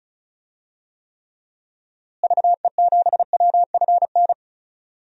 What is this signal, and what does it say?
Morse code, heard as a keyed tone, about 36 wpm, 700 Hz. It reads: VE7WFN
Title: DXCC-100 Call Signs - 6 Characters - Repeated 2x - 35wpm